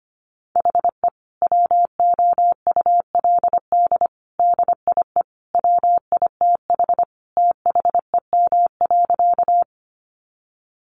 Morse code, heard as a keyed tone, about 25 wpm, 700 Hz. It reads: HE WOULD DIE WITH THEM.
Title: Sentences from Top 1300 Words 25wpm